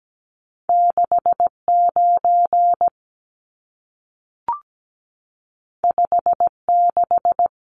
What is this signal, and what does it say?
Morse code, heard as a keyed tone, about 17 wpm, 700 Hz. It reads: 69 56
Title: Numbers - 2-Digit 17wpm